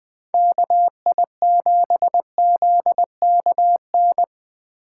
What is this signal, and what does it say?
Morse code, heard as a keyed tone, about 20 wpm, 700 Hz. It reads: KI7ZKN
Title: Speed Racing - USA Call Signs 17wpm